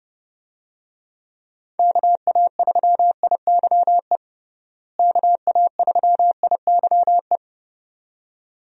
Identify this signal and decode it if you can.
Morse code, heard as a keyed tone, about 32 wpm, 700 Hz. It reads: KA3IYE KA3IYE
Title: USA Call Signs - 6 Characters - Repeated 2x - 30wpm